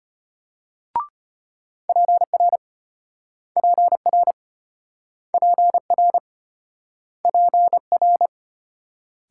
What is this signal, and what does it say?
Morse code, heard as a keyed tone, about 34 wpm, 700 Hz. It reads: PR PR PR PR